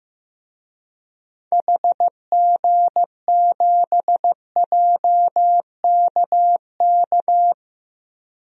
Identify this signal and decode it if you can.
Morse code, heard as a keyed tone, about 15 wpm, 700 Hz. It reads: HG7JKK
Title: DXCC-100 Call Signs - 6 Characters - Repeated 3x - 15wpm